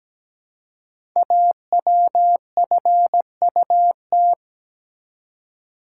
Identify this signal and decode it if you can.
Morse code, heard as a keyed tone, about 17 wpm, 700 Hz. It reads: AWFUT
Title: Code Groups - 5 Letters - 17wpm